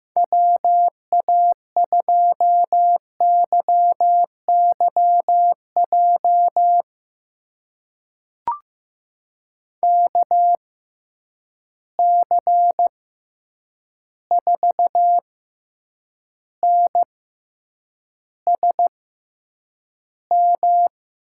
Morse code, heard as a keyed tone, about 15 wpm, spaced as for 8 wpm, 700 Hz. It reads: WA2YYJ KC4NSM
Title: USA Call Signs - 6 Characters - 6x Character Spacing - 15wpm